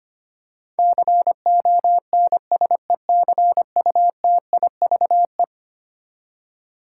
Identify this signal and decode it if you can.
Morse code, heard as a keyed tone, about 25 wpm, 700 Hz. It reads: CONSECUTIVE
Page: morse-code-ninja-practice-25wpm